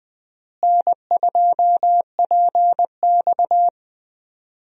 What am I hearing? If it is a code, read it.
Morse code, heard as a keyed tone, about 20 wpm, 700 Hz. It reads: N2PX
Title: USA Call Signs - 4 Characters - Repeated 3x - 20wpm